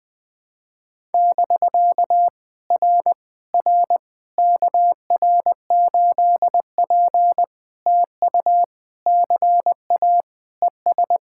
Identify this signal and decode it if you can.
Morse code, heard as a keyed tone, about 20 wpm, 700 Hz. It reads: <BK> R R KR8P TU CA ES
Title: POTA QSO Practice - 20wpm